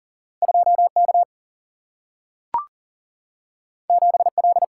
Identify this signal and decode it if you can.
Morse code, heard as a keyed tone, about 40 wpm, 700 Hz. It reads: 2K 7L
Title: Single-Letter Single-Number 40wpm